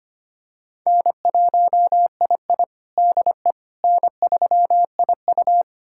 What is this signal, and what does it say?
Morse code, heard as a keyed tone, about 25 wpm, 700 Hz. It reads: N1II DE N3IU
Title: USA Callsign DE Callsign - 4 Characters - 25wpm